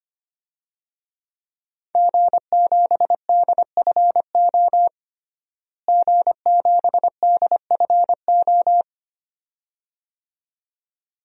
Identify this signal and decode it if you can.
Morse code, heard as a keyed tone, about 25 wpm, 700 Hz. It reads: G7DFO G7DFO
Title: DXCC-100 Call Signs - 5 Characters - Repeated 2x - 25wpm